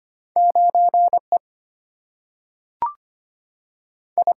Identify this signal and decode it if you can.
Morse code, heard as a keyed tone, about 25 wpm, 700 Hz. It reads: 9 E I
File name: Two Letter-Number Combinations - 25wpm